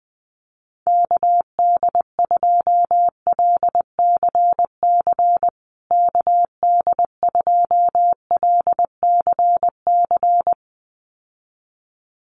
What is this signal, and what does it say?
Morse code, heard as a keyed tone, about 19 wpm, 700 Hz. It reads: KD2LCC KD2LCC
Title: USA Call Signs Repeated 2x 20wpm